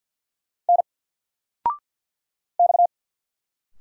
Morse code, heard as a keyed tone, about 49 wpm, 700 Hz. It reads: N X
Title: Single Letter-Number Alt 50wpm